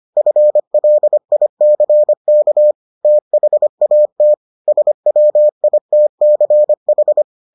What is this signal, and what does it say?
Morse code, heard as a keyed tone, about 25 wpm, 600 Hz. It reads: FLICK THAT SWITCH